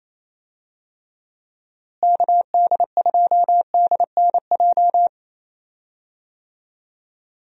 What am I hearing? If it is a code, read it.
Morse code, heard as a keyed tone, about 27 wpm, 700 Hz. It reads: KD2DNJ
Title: USA Call Signs - 6 Characters - Repeated 2x - 28wpm